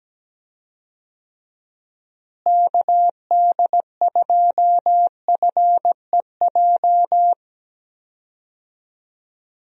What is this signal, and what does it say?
Morse code, heard as a keyed tone, about 17 wpm, 700 Hz. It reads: KD2FEJ
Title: USA Call Signs - 6 Characters - Repeated 2x - 17wpm